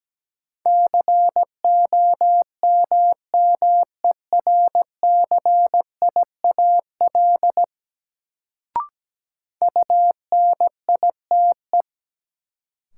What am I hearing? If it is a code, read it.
Morse code, heard as a keyed tone, about 17 wpm, 700 Hz. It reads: COMMERCIAL UNITE
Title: Sentences from Top 1000 Words - Review Part 5 - New Word Drill - 17wpm